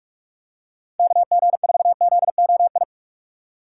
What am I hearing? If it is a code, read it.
Morse code, heard as a keyed tone, about 41 wpm, 700 Hz. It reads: KG4ZOI
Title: USA Call Signs - 6 Characters - 4x Character Spacing - 45wpm